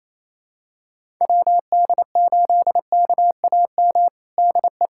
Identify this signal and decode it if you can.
Morse code, heard as a keyed tone, about 28 wpm, 700 Hz. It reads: WD8KAM DE
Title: USA Callsign DE Callsign 28wpm